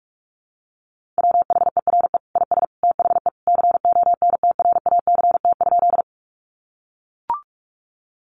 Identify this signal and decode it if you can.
Morse code, heard as a keyed tone, about 44 wpm, 700 Hz. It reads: WHERE IS THE CONTRACT?
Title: Sentences from Top 1400 Words 45wpm